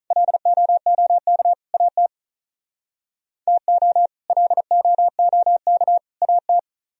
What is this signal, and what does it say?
Morse code, heard as a keyed tone, about 44 wpm, 700 Hz. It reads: LOOK AT TO LOOK AT